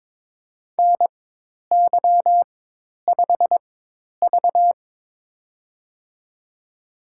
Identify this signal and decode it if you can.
Morse code, heard as a keyed tone, about 22 wpm, spaced as for 10 wpm, 700 Hz. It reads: NY5V
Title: USA Call Signs - 4 Characters - 4x Character Spacing - 22wpm